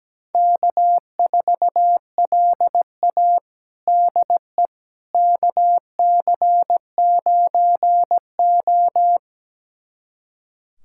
Morse code, heard as a keyed tone, about 17 wpm, 700 Hz. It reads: K4LA DE KC9O
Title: USA Callsign DE Callsign - 4 Characters - 17wpm